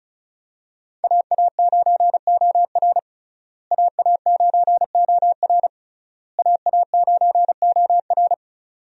Morse code, heard as a keyed tone, about 36 wpm, 700 Hz. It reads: AA9OR AA9OR AA9OR
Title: USA Call Signs - 5 Characters - Repeated 3x - 35wpm